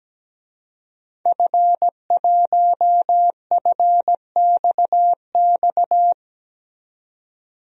Morse code, heard as a keyed tone, about 17 wpm, 700 Hz. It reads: F1FXX
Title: DXCC-100 Call Signs - 5 Characters - 17wpm